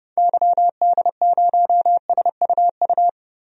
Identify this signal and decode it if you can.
Morse code, heard as a keyed tone, about 30 wpm, 700 Hz. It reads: YD0SUU
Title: DXCC-100 Call Signs - 6 Characters - Repeated 2x - 30wpm